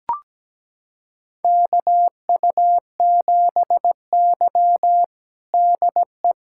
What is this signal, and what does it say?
Morse code, heard as a keyed tone, about 17 wpm, 700 Hz. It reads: KU7Y DE
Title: USA Callsign DE Callsign - 4 Characters - 17wpm